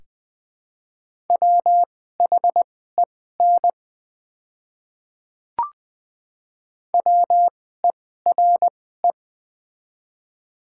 Morse code, heard as a keyed tone, about 20 wpm, spaced as for 14 wpm, 700 Hz. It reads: WHEN WERE
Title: 2x Character Spacing - Top 200 Words - 20wpm